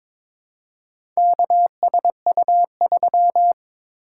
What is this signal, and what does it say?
Morse code, heard as a keyed tone, about 21 wpm, 700 Hz. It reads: KSU3
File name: practice / 22wpm / Four Letter-Number Combinations - 22wpm